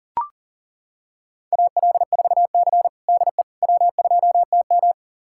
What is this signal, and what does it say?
Morse code, heard as a keyed tone, about 40 wpm, 700 Hz. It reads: AL4C DE W2TM
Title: USA Callsign DE Callsign - 4 Characters - 40wpm